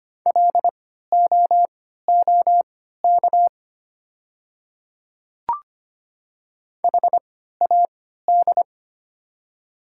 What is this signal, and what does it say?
Morse code, heard as a keyed tone, about 25 wpm, spaced as for 14 wpm, 690 Hz. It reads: LOOK HAD
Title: 3x Character Spacing - Top 100 Words - 25wpm